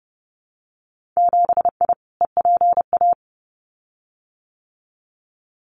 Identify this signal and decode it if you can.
Morse code, heard as a keyed tone, about 30 wpm, 700 Hz. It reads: 7I EPA
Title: Winter Field Day Exchanges Encore 30wpm